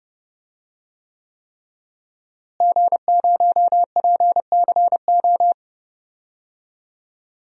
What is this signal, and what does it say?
Morse code, heard as a keyed tone, about 30 wpm, 700 Hz. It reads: G0PCO
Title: DXCC-100 Call Signs - 5 Characters - Repeated 3x - 30wpm